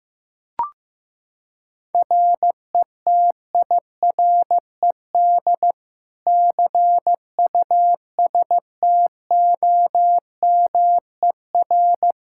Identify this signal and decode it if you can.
Morse code, heard as a keyed tone, about 15 wpm, 700 Hz. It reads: RETIRED CUSTOMER
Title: K1USN SST QSO Practice 15wpm